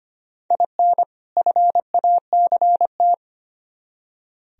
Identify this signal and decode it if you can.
Morse code, heard as a keyed tone, about 29 wpm, 700 Hz. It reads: IN FACT
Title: Speed-Racing - Sets of 2 Words 25wpm